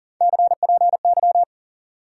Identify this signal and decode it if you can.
Morse code, heard as a keyed tone, about 40 wpm, 700 Hz. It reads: CPY
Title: Top 90 QSO Elements 40wpm